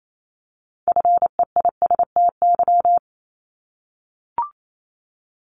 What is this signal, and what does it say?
Morse code, heard as a keyed tone, about 28 wpm, 700 Hz. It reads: FEISTY
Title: Multi-letter Phonemes 28wpm